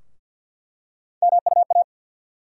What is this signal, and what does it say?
Morse code, heard as a keyed tone, about 47 wpm, 700 Hz. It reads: MUA
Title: USA Call Sign Postfixes - 3 Letters 50wpm